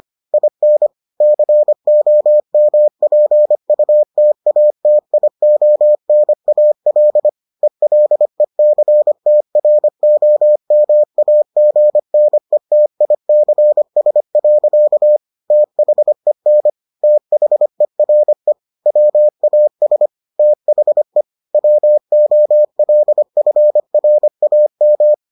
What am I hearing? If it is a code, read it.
Morse code, heard as a keyed tone, about 25 wpm, 600 Hz. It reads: IN COMPUTATIONAL ELECTROMAGNETICS. THEN THERE WAS THE WOLFRAM